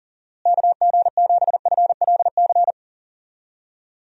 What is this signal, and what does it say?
Morse code, heard as a keyed tone, about 38 wpm, 700 Hz. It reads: KG7FLC
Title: USA Call Signs - 6 Characters - Repeated 2x - 40wpm